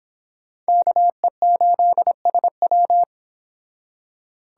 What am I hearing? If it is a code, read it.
Morse code, heard as a keyed tone, about 26 wpm, 700 Hz. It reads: KE8SW